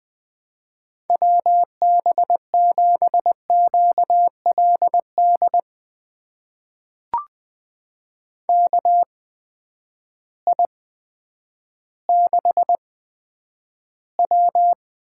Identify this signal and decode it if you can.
Morse code, heard as a keyed tone, about 20 wpm, spaced as for 10 wpm, 700 Hz. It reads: WB7QLD KI6W
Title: USA Call Signs - 6 Characters - 8x Character Spacing - 20wpm